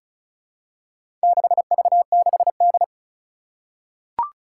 Morse code, heard as a keyed tone, about 36 wpm, 700 Hz. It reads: 6V6D
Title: Four Challenging Letter-Number Combinations 35wpm